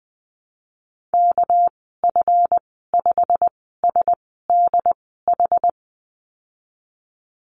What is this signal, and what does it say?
Morse code, heard as a keyed tone, about 20 wpm, spaced as for 15 wpm, 700 Hz. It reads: KF5SDH
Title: 200 Lesson 32g - Letter k - Call Signs 20wpm